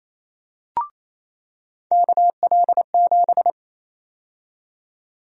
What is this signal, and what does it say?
Morse code, heard as a keyed tone, about 28 wpm, 700 Hz. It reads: KL7
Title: USA Call Sign Prefixes - 28wpm